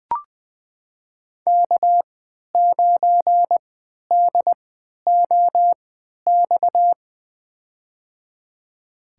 Morse code, heard as a keyed tone, about 20 wpm, spaced as for 11 wpm, 700 Hz. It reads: K9DOX
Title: USA Call Signs - 5 Characters - 3x Character Spacing - 20wpm